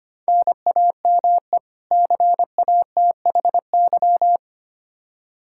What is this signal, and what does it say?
Morse code, heard as a keyed tone, about 25 wpm, 700 Hz. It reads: NAME CATHY
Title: Practice QSO - Exchange 1 - Beginning - 25wpm